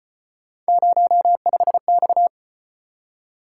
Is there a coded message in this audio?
Yes: Morse code from 0.7 to 2.3 s, heard as a keyed tone, about 33 wpm, 700 Hz. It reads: 05X